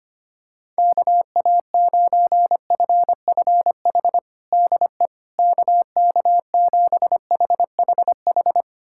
Morse code, heard as a keyed tone, about 25 wpm, 700 Hz. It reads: KA9FFH DE KK7HHH
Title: USA Callsign DE Callsign - 6 Characters - 25wpm